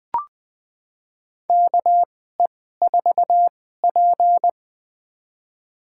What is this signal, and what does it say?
Morse code, heard as a keyed tone, about 20 wpm, spaced as for 14 wpm, 700 Hz. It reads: KE4P
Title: USA Call Signs - 4 Characters - 2x Character Spacing - 20wpm